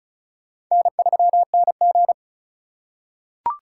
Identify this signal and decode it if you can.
Morse code, heard as a keyed tone, about 34 wpm, 700 Hz. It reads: N3NG
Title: USA Call Signs - 4 Characters - 3x Character Spacing - 35wpm